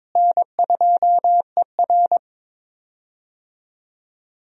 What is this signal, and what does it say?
Morse code, heard as a keyed tone, about 22 wpm, 700 Hz. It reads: N2ER
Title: USA Call Signs - 4 Characters - Repeated 3x - 22wpm